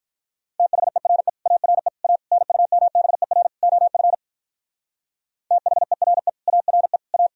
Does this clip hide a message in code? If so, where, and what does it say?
Yes: Morse code from 0.6 to 7.3 s, heard as a keyed tone, about 45 wpm, 700 Hz. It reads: THERE ARE A NUMBER OF THERE ARE A